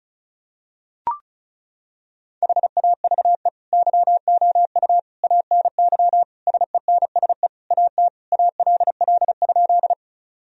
Morse code, heard as a keyed tone, about 36 wpm, 700 Hz. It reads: HAVE YOU ANY SENSE AT ALL?
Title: Sentences from Top 1000 Words - Review Part 4 - 35wpm